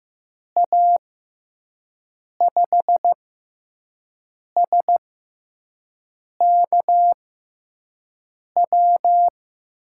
Morse code, heard as a keyed tone, about 15 wpm, spaced as for 5 wpm, 700 Hz. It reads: A5SKW
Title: USA Call Signs - 6 Characters - 6x Character Spacing - 15wpm